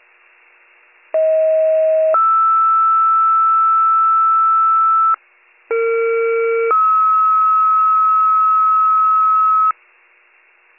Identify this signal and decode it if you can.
Two-tone page, heard as a one-second tone, then a three-second tone, about 640 then 1320 Hz, about 460 then 1290 Hz. It is heard from 1.1 to 9.7 s.